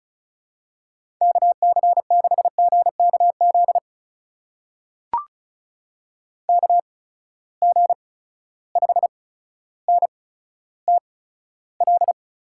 Morse code, heard as a keyed tone, about 36 wpm, spaced as for 15 wpm, 700 Hz. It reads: KC6GKZ KG5NTL
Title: USA Call Signs - 6 Characters - 8x Character Spacing - 35wpm